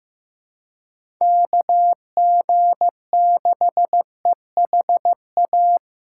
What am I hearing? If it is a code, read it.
Morse code, heard as a keyed tone, about 15 wpm, 700 Hz. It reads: KG6EHA
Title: USA Call Signs - 6 Characters - 8x Character Spacing - 15wpm